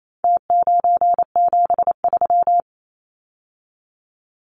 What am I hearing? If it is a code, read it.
Morse code, heard as a keyed tone, about 28 wpm, 700 Hz. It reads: T973
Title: Numbers - Contest Style - 4-Digit 28wpm